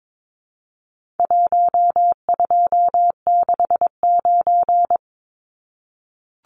Morse code, heard as a keyed tone, about 22 wpm, 700 Hz. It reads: 1269
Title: Numbers - Contest Style - 4-Digit 22wpm